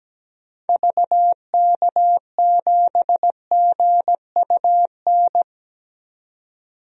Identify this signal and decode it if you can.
Morse code, heard as a keyed tone, about 17 wpm, 700 Hz. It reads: VK7GUN
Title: DXCC-100 Call Signs - 6 Characters - 17wpm